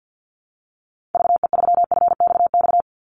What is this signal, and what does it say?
Morse code, heard as a keyed tone, about 50 wpm, 700 Hz. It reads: VE3FXX
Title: Call Signs - 50wpm